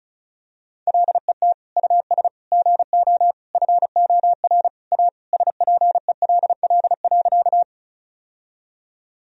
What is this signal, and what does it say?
Morse code, heard as a keyed tone, about 36 wpm, 700 Hz. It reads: LET US GO FOR A SPELL.
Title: Sentences from Top 1000 Words - Review Part 1 - 35wpm